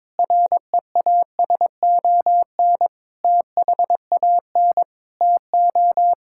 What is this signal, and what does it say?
Morse code, heard as a keyed tone, about 22 wpm, 700 Hz. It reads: REASON THAN TO
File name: practice / 22wpm / Sets of 7 Words - 22wpm